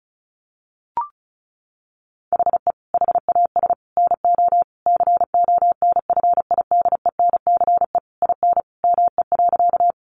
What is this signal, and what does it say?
Morse code, heard as a keyed tone, about 34 wpm, 700 Hz. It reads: HE HAS NO CONFIDENCE IN ME.
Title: Sentences from Top 1500 Words 35wpm